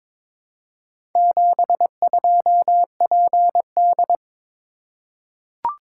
Morse code, heard as a keyed tone, about 22 wpm, 690 Hz. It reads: 72PD
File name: practice / 22wpm / Four Letter-Number Combinations - 22wpm